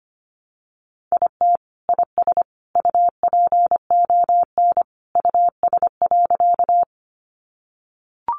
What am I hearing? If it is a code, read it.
Morse code, heard as a keyed tone, about 25 wpm, 700 Hz. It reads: IT IS UPON US.
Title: Sentences from Top 1100 Words 25wpm